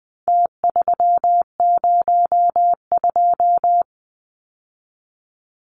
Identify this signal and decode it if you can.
Morse code, heard as a keyed tone, about 20 wpm, 700 Hz. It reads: T302
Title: Numbers - Contest Style - 4-Digit 20wpm